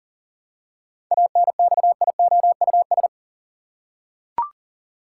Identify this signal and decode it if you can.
Morse code, heard as a keyed tone, about 40 wpm, 700 Hz. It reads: ANXIOUS
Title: Seven Letter Words - 40wpm